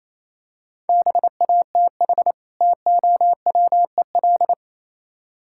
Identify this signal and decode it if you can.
Morse code, heard as a keyed tone, about 27 wpm, 700 Hz. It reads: BATH TOWEL
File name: Easy Words - Speed-Racing with Hints - 22wpm